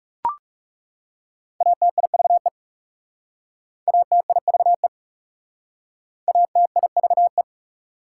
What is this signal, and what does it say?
Morse code, heard as a keyed tone, about 37 wpm, 690 Hz. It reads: ATIVE ATIVE ATIVE
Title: Speed-Racing Prefixes and Suffixes 30wpm